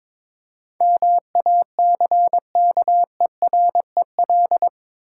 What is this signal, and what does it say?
Morse code, heard as a keyed tone, about 22 wpm, 690 Hz. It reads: MACKEREL
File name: Difficult Words with Hints - 22wpm